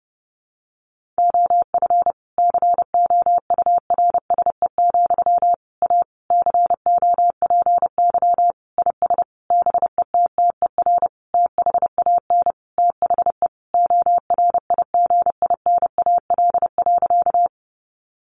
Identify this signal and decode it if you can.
Morse code, heard as a keyed tone, about 28 wpm, 700 Hz. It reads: OF COURSE, A COPY IS BETTER THAN THE ORIGINAL.
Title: Sentences from Top 1400 Words 30wpm